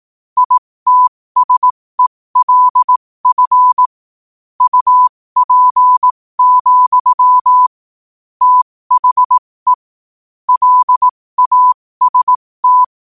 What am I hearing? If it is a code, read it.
Morse code, heard as a keyed tone, about 18 wpm, spaced as for 15 wpm, 1000 Hz. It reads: ITSELF UP, THE LAST
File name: qotdfiles / 15wpm